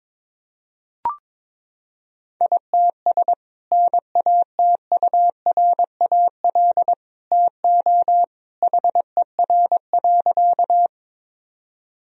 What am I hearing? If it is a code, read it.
Morse code, heard as a keyed tone, about 22 wpm, 700 Hz. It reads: ITS NATURAL TO HER.
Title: Sentences from Top 1000 Words - Review Part 4 - 22wpm